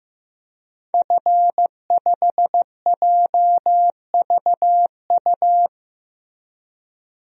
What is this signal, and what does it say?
Morse code, heard as a keyed tone, about 16 wpm, 700 Hz. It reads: F5JVU
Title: DXCC-100 Call Signs - 5 Characters - 15wpm